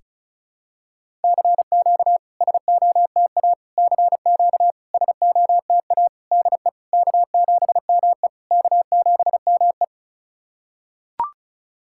Morse code, heard as a keyed tone, about 36 wpm, 690 Hz. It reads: CQ SOTA CQ SOTA DE K7ME K7ME
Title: SOTA QSO Practice 35wpm